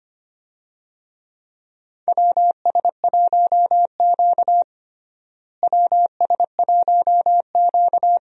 Morse code, heard as a keyed tone, about 25 wpm, 700 Hz. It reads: WS1Q WS1Q
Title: USA Call Signs - 4 Characters - Repeated 3x - 25wpm